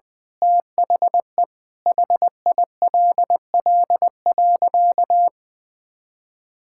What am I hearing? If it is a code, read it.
Morse code, heard as a keyed tone, about 20 wpm, 700 Hz. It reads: THE HILL.
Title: Sentences with 6-Word Phrases - Spoken Beginning - 20wpm